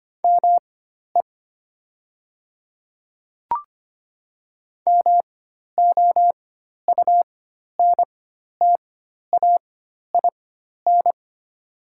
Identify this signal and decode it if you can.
Morse code, heard as a keyed tone, about 25 wpm, spaced as for 12 wpm, 700 Hz. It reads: ME MOUNTAIN
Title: 4x Character Spacing - Top 500 Words - 25wpm